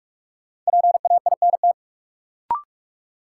Morse code, heard as a keyed tone, about 45 wpm, 700 Hz. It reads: PAINT
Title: Sentences from Top 1000 Words - Review Part 6 - New Word Drill - 45wpm